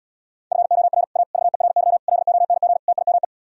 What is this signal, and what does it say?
Morse code, heard as a keyed tone, about 77 wpm, 700 Hz. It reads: FOR A VERY LONG TIME